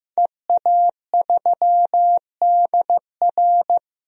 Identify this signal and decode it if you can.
Morse code, heard as a keyed tone, about 15 wpm, 700 Hz. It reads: EA3DR
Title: DXCC-100 Call Signs - 5 Characters - 15wpm